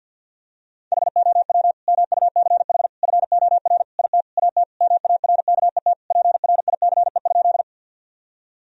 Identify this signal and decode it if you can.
Morse code, heard as a keyed tone, about 49 wpm, 700 Hz. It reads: HOW MUCH FOR IT AT MARKET PRICE?
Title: Sentences from Top 1000 Words - Review Part 3 - 50wpm